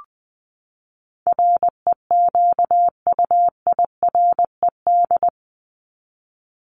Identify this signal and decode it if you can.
Morse code, heard as a keyed tone, about 20 wpm, 700 Hz. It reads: REQUIRED
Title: ED Suffix - Top 100 Words with Base Words 20wpm